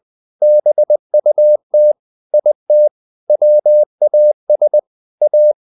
Morse code, heard as a keyed tone, about 20 wpm, 600 Hz. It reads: BUT IT WAS A
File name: the old man and the sea part 2 20wpm